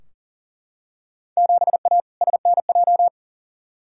Morse code, heard as a keyed tone, about 40 wpm, 700 Hz. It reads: 7A SNJ
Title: Field Day Exchanges Encore - 40wpm